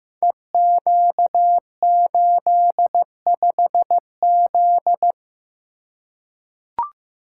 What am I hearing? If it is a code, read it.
Morse code, heard as a keyed tone, about 15 wpm, 700 Hz. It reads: EQ85Z
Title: Code Groups - 5 Letter:Numbers - 15wpm